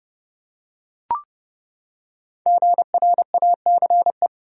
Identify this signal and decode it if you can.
Morse code, heard as a keyed tone, about 30 wpm, 700 Hz. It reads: GRACE
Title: USA Common Names - Max 5 Letters - 30wpm